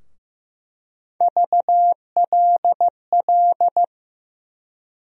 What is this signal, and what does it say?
Morse code, heard as a keyed tone, about 15 wpm, 700 Hz. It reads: VLL
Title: USA Call Sign Postfixes - 3 Letters 15wpm